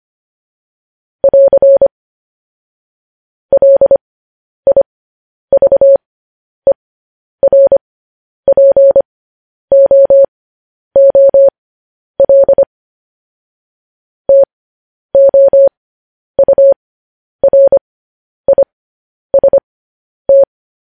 Morse code, heard as a keyed tone, about 25 wpm, spaced as for 10 wpm, 550 Hz. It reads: <AR> LIVERPOOL TOURIST
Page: News Headlines in Morse Code at 10 WPM